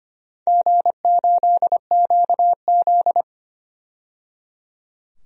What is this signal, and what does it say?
Morse code, heard as a keyed tone, about 25 wpm, 700 Hz. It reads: G8QZ
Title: DXCC-100 Call Signs - 4 Characters - 25wpm